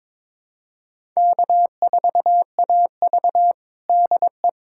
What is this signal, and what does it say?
Morse code, heard as a keyed tone, about 21 wpm, 700 Hz. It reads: K4AV DE
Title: USA Callsign DE Callsign - 4 Characters - 22wpm